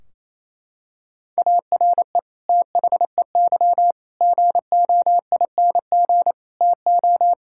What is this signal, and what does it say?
Morse code, heard as a keyed tone, about 28 wpm, 700 Hz. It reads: ARE THEY GOING TO
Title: Sentences from Top 1600 Words 28wpm